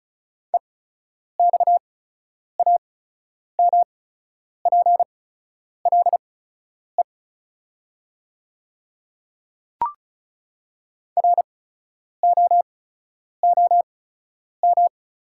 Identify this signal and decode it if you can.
Morse code, heard as a keyed tone, about 34 wpm, spaced as for 10 wpm, 700 Hz. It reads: EXAMPLE ROOM
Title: 8x Character Spacing - Top 300 Words - 35wpm